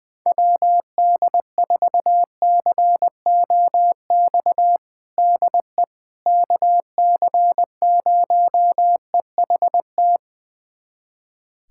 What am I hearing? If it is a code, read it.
Morse code, heard as a keyed tone, about 21 wpm, 700 Hz. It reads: WD4COX DE KC0EHT
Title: USA Callsign DE Callsign 20wpm